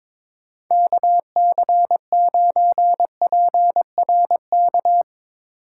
Morse code, heard as a keyed tone, about 22 wpm, 700 Hz. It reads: KC9PRK
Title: Speed Racing - USA 2x3 Call Signs 22wpm